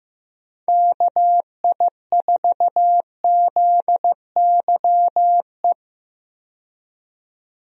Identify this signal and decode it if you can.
Morse code, heard as a keyed tone, about 15 wpm, 700 Hz. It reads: KI4ZYE
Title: USA Call Signs - 6 Characters - 15wpm